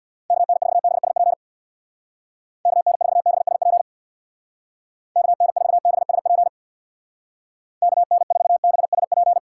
Morse code, heard as a keyed tone, about 64 wpm, 700 Hz. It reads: KN4BSP KN4BSP KN4BSP KN4BSP